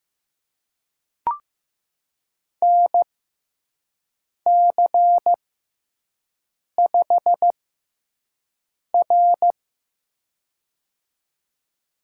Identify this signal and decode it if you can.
Morse code, heard as a keyed tone, about 15 wpm, spaced as for 5 wpm, 700 Hz. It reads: NC5R
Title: USA Call Signs - 4 Characters - 6x Character Spacing - 15wpm